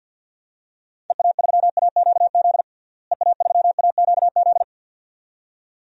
Morse code, heard as a keyed tone, about 52 wpm, 700 Hz. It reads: EA3AQZ EA3AQZ
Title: DXCC-100 Call Signs - 6 Characters - Repeated 2x - 50wpm